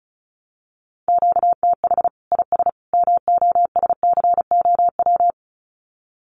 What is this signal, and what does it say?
Morse code, heard as a keyed tone, about 34 wpm, 700 Hz. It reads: QTH IS MOSCOW
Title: Common QSO Phrases 35wpm